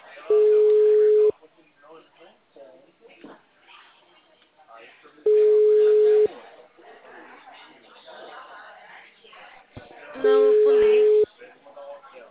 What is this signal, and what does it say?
Ringback tone (Europe, 425 Hz), with speech.